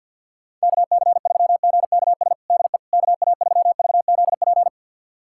Morse code, heard as a keyed tone, about 52 wpm, 700 Hz. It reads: KC3GKS DE KA3VZP
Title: USA Callsign DE Callsign - 6 Characters - 50wpm